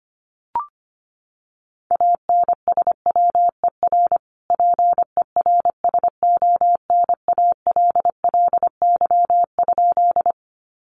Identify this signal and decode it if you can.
Morse code, heard as a keyed tone, about 25 wpm, 700 Hz. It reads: ANSWER PERSONALLY?
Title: Sentences from Top 1400 Words 25wpm